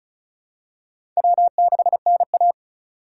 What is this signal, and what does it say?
Morse code, heard as a keyed tone, about 36 wpm, 700 Hz. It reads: W6NA
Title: USA Call Signs - 4 Characters - 4x Character Spacing - 35wpm